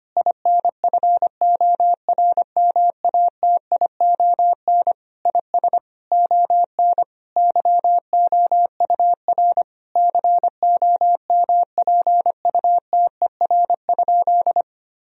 Morse code, heard as a keyed tone, about 25 wpm, 700 Hz. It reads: INFORMATION IS ON YOUR COMPUTER?
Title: Sentences from Top 1000 Words - Review Part 3 - 25wpm